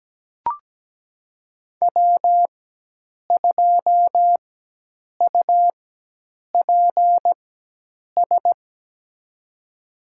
Morse code, heard as a keyed tone, about 17 wpm, spaced as for 8 wpm, 700 Hz. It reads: W2UPS
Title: USA Call Signs - 5 Characters - 4x Character Spacing - 17wpm